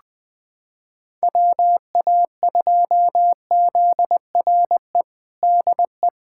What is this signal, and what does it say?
Morse code, heard as a keyed tone, about 20 wpm, 700 Hz. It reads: WA2ZRE DE
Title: USA Callsign DE Callsign 20wpm